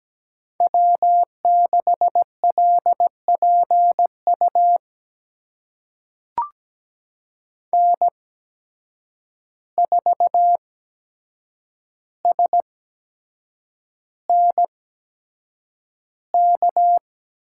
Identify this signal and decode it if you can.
Morse code, heard as a keyed tone, about 17 wpm, spaced as for 7 wpm, 700 Hz. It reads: W6LPU N4SNK
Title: USA Call Signs - 5 Characters - 8x Character Spacing - 17wpm